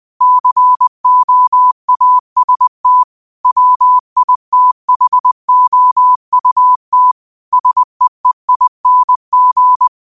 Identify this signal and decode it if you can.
Morse code, heard as a keyed tone, about 20 wpm, 1000 Hz. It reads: COAST WITHOUT SEEING
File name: qotdfiles / 20wpm